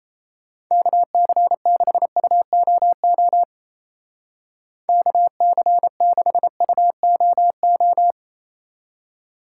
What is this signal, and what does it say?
Morse code, heard as a keyed tone, about 34 wpm, 700 Hz. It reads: KC6UOO KC6UOO
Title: Speed Racing - USA 2x3 Call Signs 28wpm